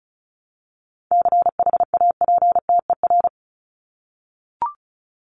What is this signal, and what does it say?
Morse code, heard as a keyed tone, about 34 wpm, 700 Hz. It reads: CHAPTER
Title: Top 1751-2000 Words 35wpm